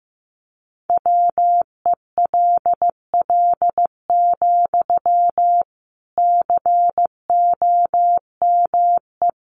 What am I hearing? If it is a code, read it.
Morse code, heard as a keyed tone, about 15 wpm, 700 Hz. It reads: WELL, COME
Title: Sentences from Top 1000 Words 15wpm